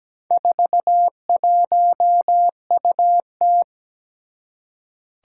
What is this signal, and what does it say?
Morse code, heard as a keyed tone, about 17 wpm, 700 Hz. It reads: 41UT
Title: Four Challenging Letter-Number Combinations 17wpm